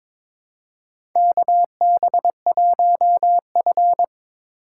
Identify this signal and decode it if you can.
Morse code, heard as a keyed tone, about 22 wpm, 700 Hz. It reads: KB1F
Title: USA Call Signs - 4 Characters - 4x Character Spacing - 22wpm